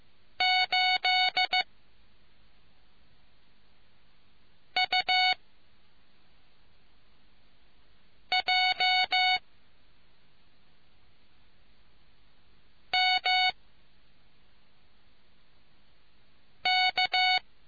Morse code, heard as a keyed tone, about 15 wpm, spaced as for 3 wpm, 2200 Hz. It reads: 8UJMK